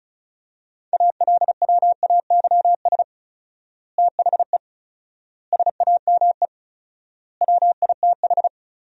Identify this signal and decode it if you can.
Morse code, heard as a keyed tone, about 36 wpm, 700 Hz. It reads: ALWAYS THE SAME WITH
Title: 4x Word-Spacing - Sentences from Top 300 Words - 35wpm